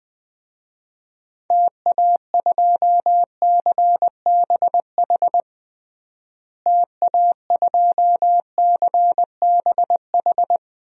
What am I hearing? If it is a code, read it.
Morse code, heard as a keyed tone, about 20 wpm, 700 Hz. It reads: TA2CBH TA2CBH
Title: DXCC-100 Call Signs - 6 Characters - Repeated 2x - 20wpm